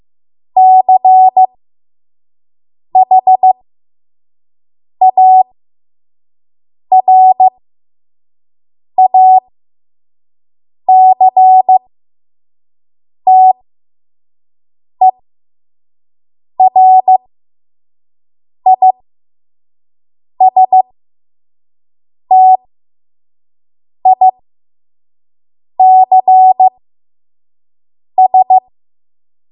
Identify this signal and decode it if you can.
Morse code, heard as a keyed tone, about 15 wpm, spaced as for 5 wpm, 750 Hz. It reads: CHARACTERISTICS
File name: Morse / 05WPM